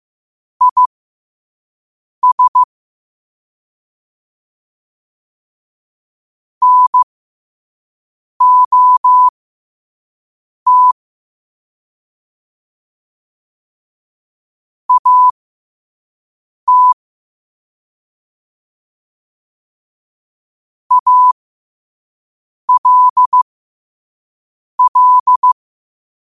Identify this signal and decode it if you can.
Morse code, heard as a keyed tone, about 15 wpm, spaced as for 5 wpm, 1000 Hz. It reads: IS NOT AT ALL
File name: qotdfiles / 5wpm